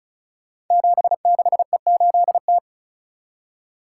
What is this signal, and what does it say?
Morse code, heard as a keyed tone, about 35 wpm, 700 Hz. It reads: 76E8T
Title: Code Groups - 5 Letter:Numbers - 35wpm